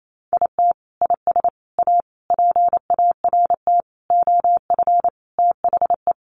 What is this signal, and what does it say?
Morse code, heard as a keyed tone, about 28 wpm, 700 Hz. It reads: IT IS A PART OF THE
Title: Sentences from Top 1400 Words 28wpm